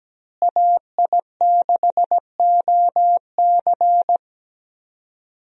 Morse code, heard as a keyed tone, about 17 wpm, 700 Hz. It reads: AI6OC
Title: USA Call Signs - 5 Characters - Repeated 2x - 17wpm